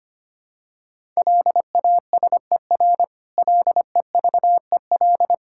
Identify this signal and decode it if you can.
Morse code, heard as a keyed tone, about 25 wpm, 700 Hz. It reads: LASER LEVEL
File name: practice / 25wpm / Difficult Words with Hints - 25wpm